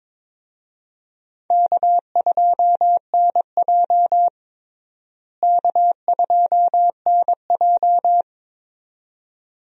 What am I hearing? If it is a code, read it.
Morse code, heard as a keyed tone, about 22 wpm, 700 Hz. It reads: K2NJ K2NJ
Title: USA Call Signs - 4 Characters - Repeated 2x - 22wpm